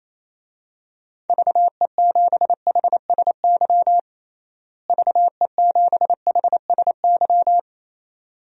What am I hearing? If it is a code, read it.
Morse code, heard as a keyed tone, about 28 wpm, 700 Hz. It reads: VE7HSY VE7HSY
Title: DXCC-100 Call Signs - 6 Characters - Repeated 2x - 28wpm